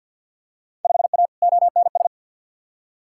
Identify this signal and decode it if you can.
Morse code, heard as a keyed tone, about 46 wpm, 700 Hz. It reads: 5A ONS
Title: Field Day Exchanges Encore - 50wpm